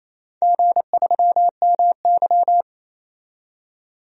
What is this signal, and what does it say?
Morse code, heard as a keyed tone, about 28 wpm, 700 Hz. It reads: G3MY
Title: DXCC-100 Call Signs - 4 Characters - 28wpm